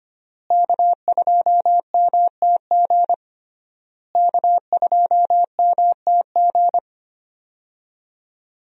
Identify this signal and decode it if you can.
Morse code, heard as a keyed tone, about 25 wpm, 700 Hz. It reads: K2MTG K2MTG
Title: USA Call Signs - 5 Characters - Repeated 3x - 25wpm